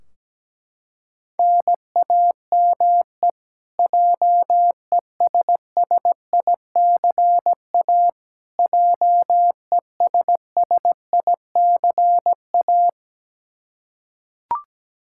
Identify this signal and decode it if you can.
Morse code, heard as a keyed tone, about 17 wpm, 700 Hz. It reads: NAME JESSICA JESSICA
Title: Practice QSO - Exchange 1 - Beginning - 17wpm